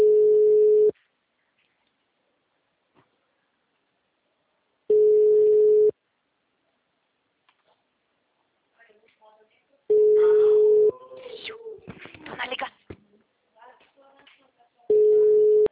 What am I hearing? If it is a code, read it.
Ringback tone (Europe, 425 Hz), with speech.
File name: out- -2009- - - .193